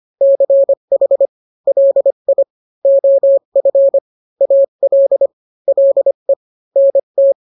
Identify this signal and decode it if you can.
Morse code, heard as a keyed tone, about 25 wpm, 550 Hz. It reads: CH LI OF AL LE NT